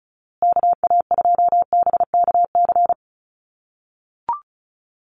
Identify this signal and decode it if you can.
Morse code, heard as a keyed tone, about 34 wpm, 700 Hz. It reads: KA2BKC
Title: USA Call Signs Repeated 3x 35wpm